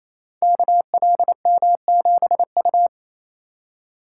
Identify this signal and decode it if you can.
Morse code, heard as a keyed tone, about 28 wpm, 700 Hz. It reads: KLM7U
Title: Code Groups - 5 Letter:Numbers - 28wpm